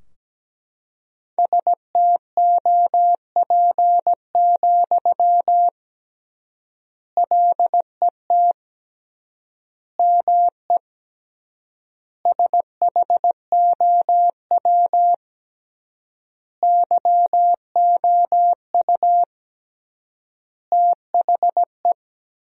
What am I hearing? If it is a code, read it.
Morse code, heard as a keyed tone, about 17 wpm, 700 Hz. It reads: STOP, LET ME SHOW YOU THE
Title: 3x Word-Spacing - Sentences from Top 300 Words - 17wpm